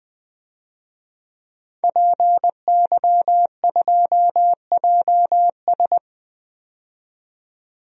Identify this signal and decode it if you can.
Morse code, heard as a keyed tone, about 20 wpm, 700 Hz. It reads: PY2JS
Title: DXCC-100 Call Signs - 5 Characters - 20wpm